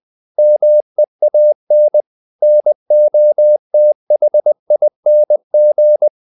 Morse code, heard as a keyed tone, about 20 wpm, 600 Hz. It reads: MEAN NOTHING
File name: the old man and the sea part 2 20wpm